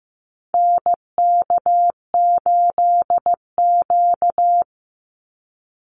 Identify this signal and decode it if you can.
Morse code, heard as a keyed tone, about 15 wpm, 700 Hz. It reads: NK8Q
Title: NA SOTA Activator Callsigns 15wpm